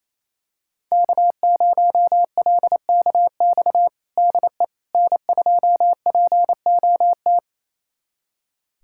Morse code, heard as a keyed tone, about 28 wpm, 700 Hz. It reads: K0LKX DE N2POT
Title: USA Callsign DE Callsign - 5 Characters - 28wpm